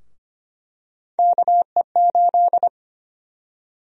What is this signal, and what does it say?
Morse code, heard as a keyed tone, about 25 wpm, 700 Hz. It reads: KE8
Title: USA Call Sign Prefixes - Repeated 2x - 25wpm